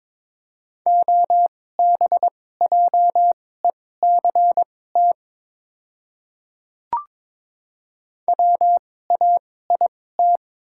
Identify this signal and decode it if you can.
Morse code, heard as a keyed tone, about 22 wpm, spaced as for 16 wpm, 700 Hz. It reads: OBJECT WAIT
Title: 2x Character Spacing - Top 500 Words - 22wpm